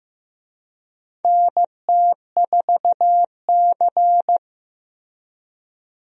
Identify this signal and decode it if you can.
Morse code, heard as a keyed tone, about 15 wpm, 700 Hz. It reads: NT4C
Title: USA Call Signs - 4 Characters - Repeated 2x - 15wpm